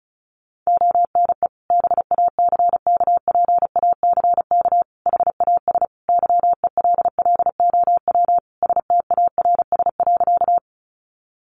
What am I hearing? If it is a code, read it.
Morse code, heard as a keyed tone, about 34 wpm, 700 Hz. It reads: ONE BACKPACK HAS YELLOW STARS.
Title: Additional Easy Phrases 35wpm